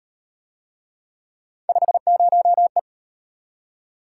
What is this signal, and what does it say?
Morse code, heard as a keyed tone, about 34 wpm, 700 Hz. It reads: 50E